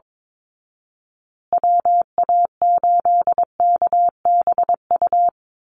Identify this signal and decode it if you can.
Morse code, heard as a keyed tone, about 22 wpm, 700 Hz. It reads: WA8KBU
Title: Call Signs Repeated 2x - 22wpm